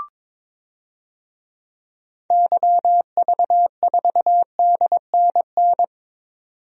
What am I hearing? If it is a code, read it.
Morse code, heard as a keyed tone, about 21 wpm, 700 Hz. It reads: YV4DNN
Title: DXCC-100 Call Signs - 6 Characters - 22wpm